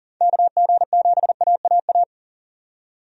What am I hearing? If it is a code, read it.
Morse code, heard as a keyed tone, about 40 wpm, 700 Hz. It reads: KG7AAA
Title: USA Call Signs - 6 Characters - 40wpm